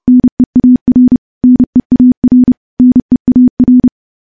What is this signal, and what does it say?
Morse code, heard as a keyed tone, about 32 wpm, 250 Hz. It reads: NEAR NEAR NEAR
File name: morse / morse-training-30WPM